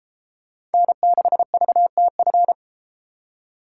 Morse code, heard as a keyed tone, about 32 wpm, 700 Hz. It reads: N6VTF